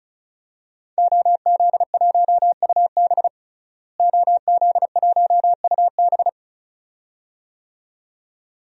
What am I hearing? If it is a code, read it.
Morse code, heard as a keyed tone, about 34 wpm, 700 Hz. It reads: OZ1UB OZ1UB
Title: DXCC-100 Call Signs - 5 Characters - Repeated 2x - 35wpm